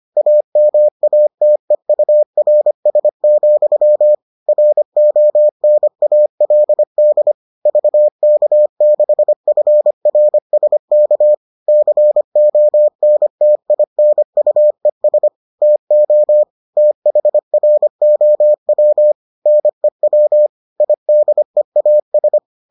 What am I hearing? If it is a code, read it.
Morse code, heard as a keyed tone, about 25 wpm, 600 Hz. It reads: AMATEURS, RONALD VK6FRSK CONTINUES TO THROW NEW IDEAS